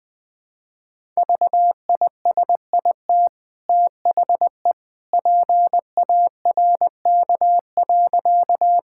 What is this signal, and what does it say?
Morse code, heard as a keyed tone, about 20 wpm, 700 Hz. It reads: VISIT THE PARK.
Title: Easy Sentences with Hints - 20wpm